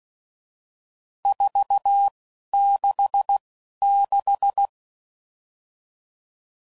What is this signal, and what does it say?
Morse code, heard as a keyed tone, about 16 wpm, spaced as for 11 wpm, 800 Hz. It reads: 466
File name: CW Code Groups @10WPM / RSTs / Various RST No's